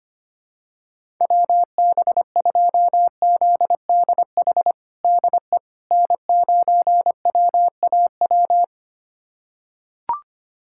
Morse code, heard as a keyed tone, about 25 wpm, 700 Hz. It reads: WB2ZDH DE N9WAW
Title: USA Callsign DE Callsign 25wpm